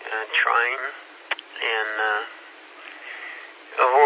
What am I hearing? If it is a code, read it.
Emergency Alert System (SAME) tones, heard as harsh buzzing bursts, with speech. It is heard from 0.7 to 3.4 s.